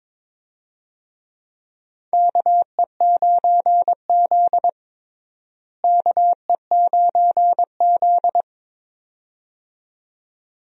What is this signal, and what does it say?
Morse code, heard as a keyed tone, about 22 wpm, 700 Hz. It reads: KE9Z KE9Z
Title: USA Call Signs - 4 Characters - Repeated 2x - 22wpm